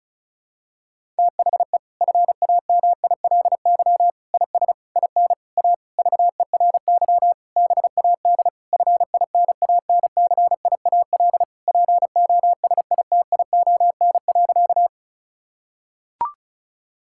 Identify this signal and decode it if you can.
Morse code, heard as a keyed tone, about 36 wpm, 700 Hz. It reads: THE FAMILY IS IN A VERY BAD FINANCIAL POSITION.
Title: Sentences from Top 1000 Words - Review Part 5 - 35wpm